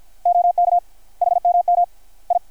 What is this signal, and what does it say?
Morse code, heard as a keyed tone, about 53 wpm, 720 Hz. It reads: OK HMM I
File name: archived / CRSnet